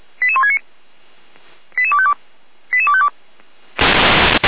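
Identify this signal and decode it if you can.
Five-tone selective call (ZVEI), heard as quick runs of five tones, tones of about 70 ms. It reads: 88169 88252 88252
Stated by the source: ZVEI five-tone call 88166 received by SDR.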